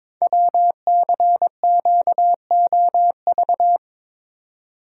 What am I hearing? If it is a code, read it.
Morse code, heard as a keyed tone, about 22 wpm, 700 Hz. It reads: WCQOV
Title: Code Groups - 5 Letters - 22wpm